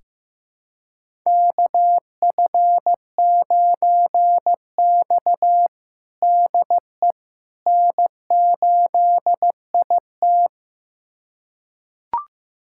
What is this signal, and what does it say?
Morse code, heard as a keyed tone, about 15 wpm, 700 Hz. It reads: KF9X DE N8IT
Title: USA Callsign DE Callsign - 4 Characters - 15wpm